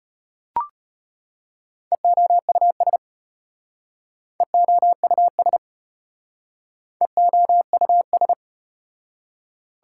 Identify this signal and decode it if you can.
Morse code, heard as a keyed tone, about 34 wpm, 700 Hz. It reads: EOUS EOUS EOUS